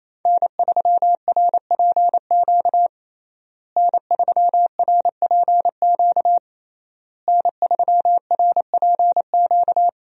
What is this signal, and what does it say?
Morse code, heard as a keyed tone, about 28 wpm, 700 Hz. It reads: N3RPQ N3RPQ N3RPQ
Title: USA Call Signs - 5 Characters - Repeated 3x - 28wpm